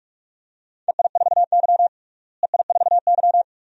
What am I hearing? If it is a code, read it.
Morse code, heard as a keyed tone, about 41 wpm, 700 Hz. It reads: EI4Y EI4Y
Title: DXCC-100 Call Signs - 4 Characters - Repeated 2x - 45wpm